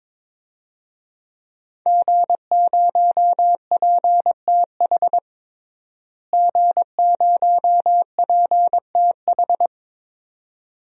Morse code, heard as a keyed tone, about 22 wpm, 700 Hz. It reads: G0PTH G0PTH
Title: DXCC-100 Call Signs - 5 Characters - Repeated 2x - 22wpm